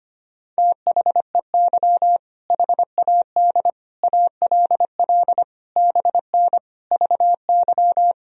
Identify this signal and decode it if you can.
Morse code, heard as a keyed tone, about 25 wpm, 700 Hz. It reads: THEY HAD ALL BN VY
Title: Sentences from Top 200 Words and Abbreviations 25wpm